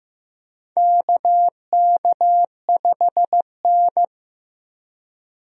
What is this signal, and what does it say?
Morse code, heard as a keyed tone, about 15 wpm, 700 Hz. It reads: KK5N
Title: USA Call Signs - 4 Characters - 15wpm